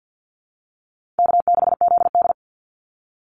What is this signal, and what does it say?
Morse code, heard as a keyed tone, about 50 wpm, 700 Hz. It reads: K6ZD
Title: Call Signs - 50wpm